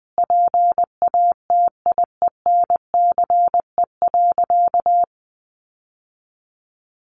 Morse code, heard as a keyed tone, about 20 wpm, 700 Hz. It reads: PATIENCE.
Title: Sentences from Top 1300 Words 20wpm